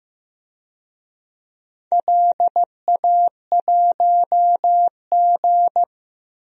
Morse code, heard as a keyed tone, about 15 wpm, 700 Hz. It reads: LA1G
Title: DXCC-100 Call Signs - 4 Characters - 15wpm